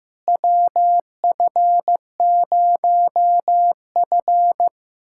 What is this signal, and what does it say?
Morse code, heard as a keyed tone, about 15 wpm, 700 Hz. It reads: WF0F
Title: USA Callsign DE Callsign - 4 Characters - 15wpm